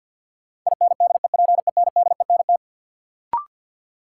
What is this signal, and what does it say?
Morse code, heard as a keyed tone, about 52 wpm, 700 Hz. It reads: INDEPENDENT
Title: Sentences from Top 900 Words - New Word Drill - 50wpm